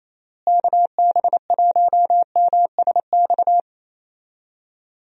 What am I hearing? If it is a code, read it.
Morse code, heard as a keyed tone, about 28 wpm, 700 Hz. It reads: KB1MSX
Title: USA Call Signs - 6 Characters - Repeated 3x - 28wpm